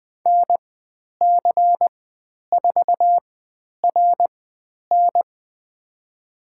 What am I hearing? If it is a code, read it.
Morse code, heard as a keyed tone, about 20 wpm, spaced as for 10 wpm, 700 Hz. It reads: NC4RN
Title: 109 Lesson 18g - Letter m - Call Signs 20wpm